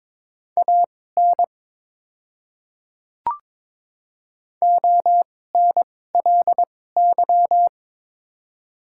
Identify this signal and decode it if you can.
Morse code, heard as a keyed tone, about 21 wpm, spaced as for 16 wpm, 700 Hz. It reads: AN ONLY
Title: 2x Character Spacing - Top 500 Words - 22wpm